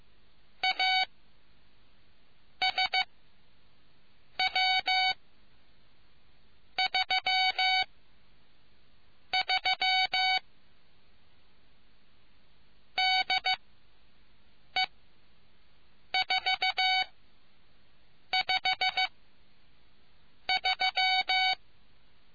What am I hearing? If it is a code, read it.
Morse code, heard as a keyed tone, about 15 wpm, spaced as for 5 wpm, 750 Hz. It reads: ASW33 DE453